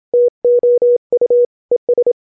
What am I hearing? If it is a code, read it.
Morse code, heard as a keyed tone, about 27 wpm, 490 Hz. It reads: TO U ES